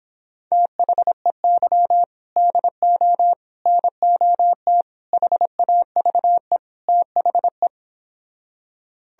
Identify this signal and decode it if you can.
Morse code, heard as a keyed tone, about 31 wpm, 700 Hz. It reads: THEY DO NOT HAVE THE